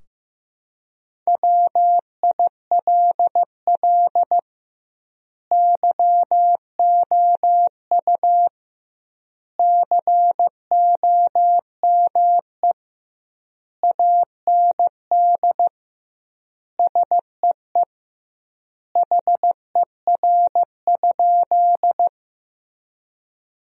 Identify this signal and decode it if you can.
Morse code, heard as a keyed tone, about 15 wpm, 700 Hz. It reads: WILL YOU COME AND SEE HER?
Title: 2x Word-Spacing - Sentences from Top 100 Words - 15wpm